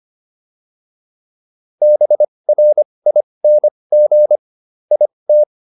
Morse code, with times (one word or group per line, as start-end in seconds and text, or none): 1.8-4.4 BRING
4.9-5.4 IT